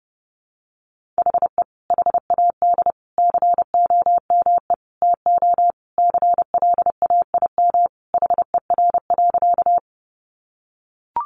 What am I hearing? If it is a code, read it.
Morse code, heard as a keyed tone, about 30 wpm, 700 Hz. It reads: HE HAD COME TO CLAIM HER.
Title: Sentences from Top 1500 Words 30wpm